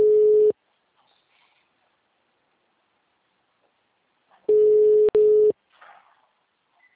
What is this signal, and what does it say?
Ringback tone (Europe, 425 Hz).